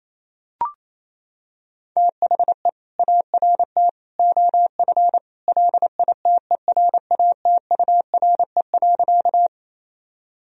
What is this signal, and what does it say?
Morse code, heard as a keyed tone, about 28 wpm, 700 Hz. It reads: THE ART OF LITERATURE.
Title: Sentences from Top 1000 Words - Review Part 3 - 28wpm